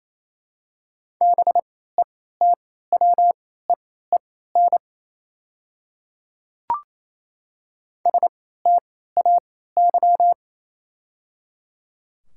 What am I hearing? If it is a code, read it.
Morse code, heard as a keyed tone, about 28 wpm, spaced as for 16 wpm, 700 Hz. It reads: BETWEEN STAY
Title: 3x Character Spacing - Top 500 Words - 28wpm